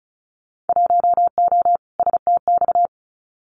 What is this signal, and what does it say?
Morse code, heard as a keyed tone, about 36 wpm, 700 Hz. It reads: 1O STX
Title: Winter Field Day Exchanges 35wpm